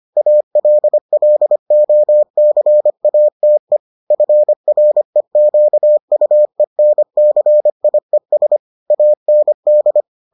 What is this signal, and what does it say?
Morse code, heard as a keyed tone, about 25 wpm, 600 Hz. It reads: ALLOCATE FREQUENCIES AND